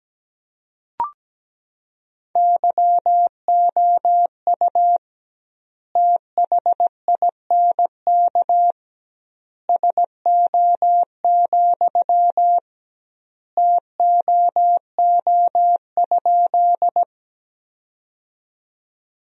Morse code, heard as a keyed tone, about 17 wpm, 700 Hz. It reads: YOU THINK SO, TOO?
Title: 2x Word-Spacing - Sentences from Top 200 Words - 17wpm